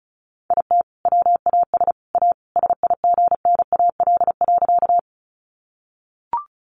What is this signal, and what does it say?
Morse code, heard as a keyed tone, about 34 wpm, 700 Hz. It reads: IT WAS A SIGNAL.
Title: Sentences from Top 1400 Words 35wpm